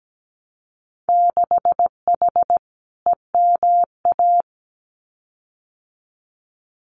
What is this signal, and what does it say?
Morse code, heard as a keyed tone, about 17 wpm, 700 Hz. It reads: 6H EMA
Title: Winter Field Day Exchanges Encore 17wpm